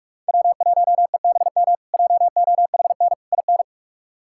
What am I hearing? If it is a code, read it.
Morse code, heard as a keyed tone, about 46 wpm, 700 Hz. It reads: W1EBM JOHN IN
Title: K1USN SST QSO Practice - 45wpm